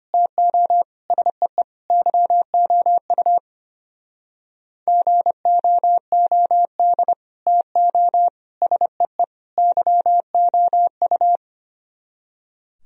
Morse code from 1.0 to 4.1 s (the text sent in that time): SEE YOU